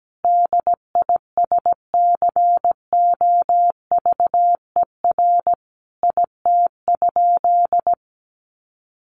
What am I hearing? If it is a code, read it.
Morse code, heard as a keyed tone, about 17 wpm, 700 Hz. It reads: DISCOVER IT?
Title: Sentences from Top 1100 Words 17wpm